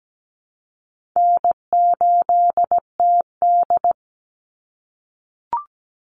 Morse code, heard as a keyed tone, about 17 wpm, 700 Hz. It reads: N8TD
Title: Call Signs - 17wpm